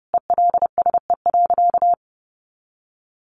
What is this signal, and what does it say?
Morse code, heard as a keyed tone, about 30 wpm, 700 Hz. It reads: ELSE.
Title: Sentences from Top 800 Words 30wpm